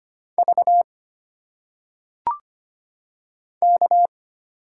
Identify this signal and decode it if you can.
Morse code, heard as a keyed tone, about 25 wpm, 700 Hz. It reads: V K
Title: Single Letters - Phonetic Pronunciation 25wpm